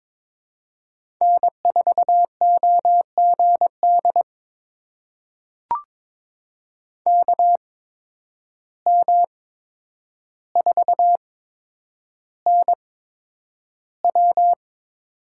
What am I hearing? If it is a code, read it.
Morse code, heard as a keyed tone, about 22 wpm, spaced as for 9 wpm, 700 Hz. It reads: N4OGD KM4NW
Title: USA Call Signs - 5 Characters - 8x Character Spacing - 22wpm